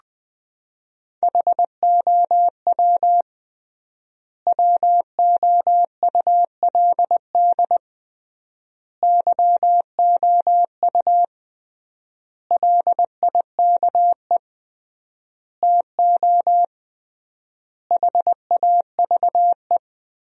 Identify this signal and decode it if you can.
Morse code, heard as a keyed tone, about 20 wpm, 700 Hz. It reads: HOW WOULD YOU LIKE TO HAVE
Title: 3x Word-Spacing - Sentences from Top 100 Words - 20wpm